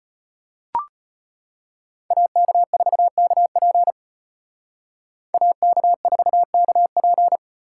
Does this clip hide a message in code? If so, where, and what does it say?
Yes: Morse code from 2.1 to 7.4 s, heard as a keyed tone, about 32 wpm, 700 Hz. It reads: AK4KP AK4KP